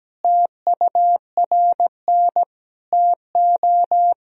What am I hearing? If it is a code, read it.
Morse code, heard as a keyed tone, about 17 wpm, 700 Hz. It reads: TURN TO
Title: Sentences from Top 200 Words and Abbreviations 17wpm